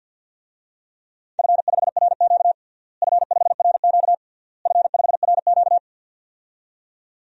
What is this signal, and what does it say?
Morse code, heard as a keyed tone, about 53 wpm, 700 Hz. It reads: F5RQ F5RQ F5RQ
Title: DXCC-100 Call Signs - 4 Characters - Repeated 3x - 50wpm